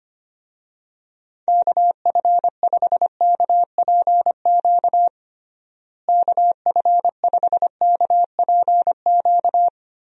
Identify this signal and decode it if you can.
Morse code, heard as a keyed tone, about 25 wpm, 700 Hz. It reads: KF5KPQ KF5KPQ
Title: USA Call Signs - 6 Characters - Repeated 2x - 25wpm